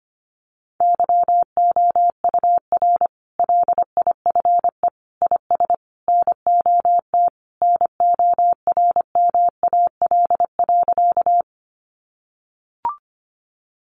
Morse code, heard as a keyed tone, about 25 wpm, 700 Hz. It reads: YOUR LIFE IS NOT NORMAL.
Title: Sentences from Top 1200 Words 25wpm